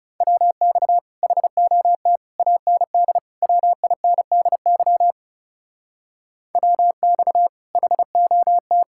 Morse code, heard as a keyed tone, about 37 wpm, 700 Hz. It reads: WX HOT AND WINDY WX HOT
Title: Speed-Racing Common QSO Phrases Encore - 30wpm